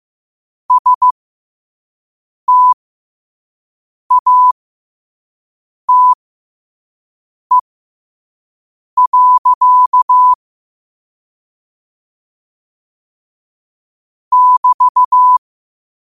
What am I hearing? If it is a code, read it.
Morse code, heard as a keyed tone, about 15 wpm, spaced as for 5 wpm, 1000 Hz. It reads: STATE. <BT>
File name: qotdfiles / 5wpm